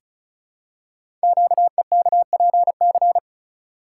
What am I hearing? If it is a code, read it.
Morse code, heard as a keyed tone, about 36 wpm, 690 Hz. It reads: QEKPC
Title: Code Groups - 5 Letters - 35wpm